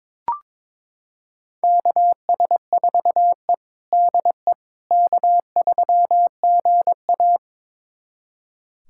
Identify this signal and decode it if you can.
Morse code, heard as a keyed tone, about 22 wpm, 700 Hz. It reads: KS4E DE K3GA
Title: USA Callsign DE Callsign - 4 Characters - 22wpm